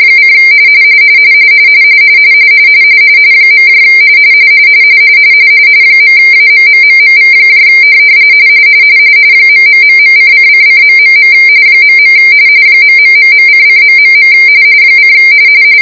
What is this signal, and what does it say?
RTTY, about 45.45 baud. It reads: : . .AMMMMV ....::::... .. ....::... :